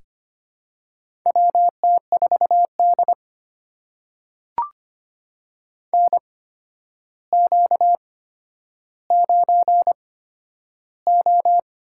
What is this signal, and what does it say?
Morse code, heard as a keyed tone, about 25 wpm, spaced as for 11 wpm, 700 Hz. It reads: WT4D NQ9O
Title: USA Call Signs - 4 Characters - 8x Character Spacing - 25wpm